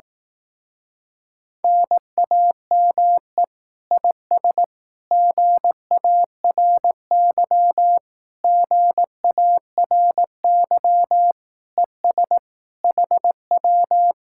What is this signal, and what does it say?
Morse code, heard as a keyed tone, about 18 wpm, 700 Hz. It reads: NAME IS GARY GARY ES HW
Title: Speed-Racing Common QSO Phrases Encore - 15wpm